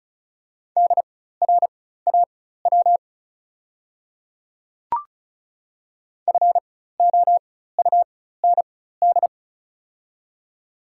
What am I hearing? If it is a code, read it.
Morse code, heard as a keyed tone, about 34 wpm, spaced as for 16 wpm, 700 Hz. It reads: DRAW FOUND
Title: 4x Character Spacing - Top 300 Words - 35wpm